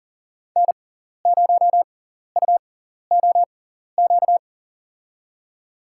Morse code, heard as a keyed tone, about 38 wpm, spaced as for 14 wpm, 700 Hz. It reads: N0UOQ
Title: USA Call Signs - 6x Character Spacing - 40wpm